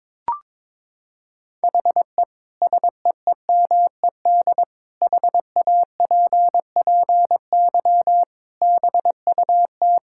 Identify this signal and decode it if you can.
Morse code, heard as a keyed tone, about 22 wpm, 700 Hz. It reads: HE SEEMED HAPPY BUT
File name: practice / 22wpm / Sentences with 8-Word Phrases - 22wpm